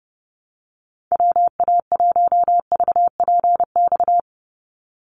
Morse code, heard as a keyed tone, about 30 wpm, 700 Hz. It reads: WA1VPX
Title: USA Call Signs 30wpm